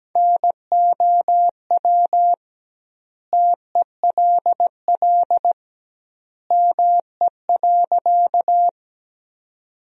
Morse code, heard as a keyed tone, about 17 wpm, 700 Hz. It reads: NOW TELL ME.
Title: 2x Word-Spacing - Sentences from Top 200 Words - 17wpm